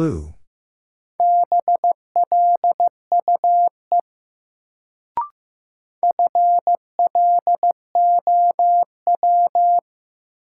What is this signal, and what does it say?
Morse code, heard as a keyed tone, about 15 wpm, 700 Hz. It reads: BLUE FLOW
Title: Sentences from Top 1200 Words - New Word Drill Alt - 15wpm